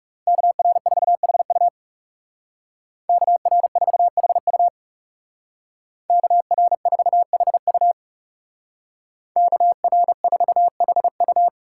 Morse code, heard as a keyed tone, about 39 wpm, 700 Hz. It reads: KR4HU KR4HU KR4HU KR4HU